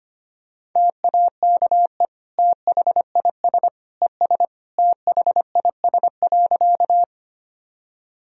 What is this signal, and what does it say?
Morse code, heard as a keyed tone, about 25 wpm, 700 Hz. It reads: TAKE THIS ES THIS.
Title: Sentences from Top 200 Words and Abbreviations 25wpm